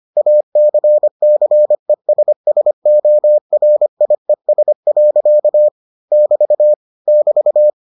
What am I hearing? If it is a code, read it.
Morse code, heard as a keyed tone, about 25 wpm, 600 Hz. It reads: ACCESSORIES. <BT> <BT>